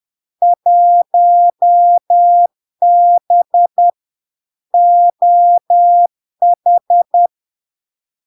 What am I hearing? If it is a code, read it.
Morse code, heard as a keyed tone, about 10 wpm, 700 Hz. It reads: 1B OH
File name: LARK-FD-2022-Log-in-CW-10wpm-700-Hz